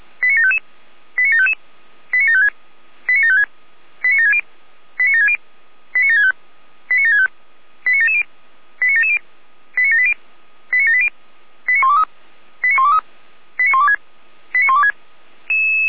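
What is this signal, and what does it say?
Five-tone selective call (ZVEI), heard as quick runs of five tones, tones of about 70 ms. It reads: 89755 89755 89756 89756 89760 89760 89765 89765 89770 89770 89780 89780 89123 89123 89126 89126
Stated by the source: ZVEI five-tone selective call received with an SDR.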